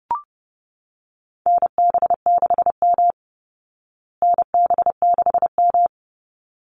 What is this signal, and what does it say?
Morse code, heard as a keyed tone, about 30 wpm, 700 Hz. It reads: NB6M NB6M
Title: Call Signs Repeated 3x - 30wpm